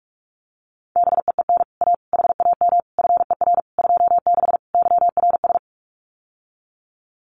Morse code, heard as a keyed tone, about 44 wpm, 700 Hz. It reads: BEEN A HAM FER 26 YRS
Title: Common QSO Phrases Encore 45wpm